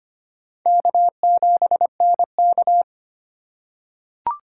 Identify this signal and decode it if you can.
Morse code, heard as a keyed tone, about 25 wpm, 700 Hz. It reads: K7NK
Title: USA Call Signs - 4 Characters - 4x Character Spacing - 25wpm